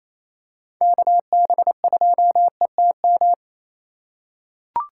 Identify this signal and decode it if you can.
Morse code, heard as a keyed tone, about 28 wpm, 690 Hz. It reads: KB2ETM
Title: USA Call Signs - 3x Character Spacing - 28wpm